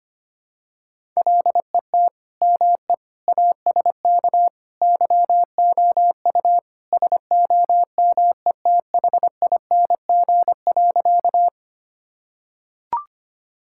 Morse code, heard as a keyed tone, about 25 wpm, 700 Hz. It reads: LET ME ASK YOU SOMETHING.
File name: practice / 25wpm / Sentences with 4-Word Phrases - 25wpm